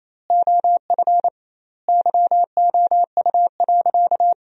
Morse code, heard as a keyed tone, about 28 wpm, 700 Hz. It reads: OF YOU.
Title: 2x Word-Spacing - Sentences from Top 200 Words - 28wpm